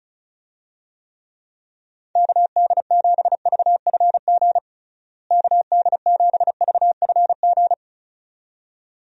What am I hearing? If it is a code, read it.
Morse code, heard as a keyed tone, about 36 wpm, 700 Hz. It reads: KD7VFG KD7VFG
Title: USA Call Signs - 6 Characters - Repeated 2x - 35wpm